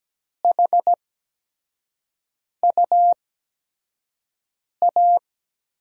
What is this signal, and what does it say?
Morse code, heard as a keyed tone, about 17 wpm, spaced as for 5 wpm, 700 Hz. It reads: HUA
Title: USA Call Signs - 8x Character Spacing - 17wpm